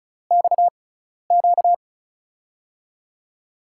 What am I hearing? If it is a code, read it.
Morse code, heard as a keyed tone, about 36 wpm, spaced as for 12 wpm, 700 Hz. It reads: XQ
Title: USA Call Signs - 5 Characters - 6x Character Spacing - 35wpm